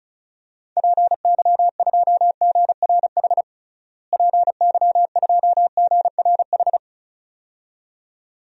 Morse code, heard as a keyed tone, about 36 wpm, 700 Hz. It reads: PY2GRH PY2GRH
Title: DXCC-100 Call Signs - 6 Characters - Repeated 2x - 35wpm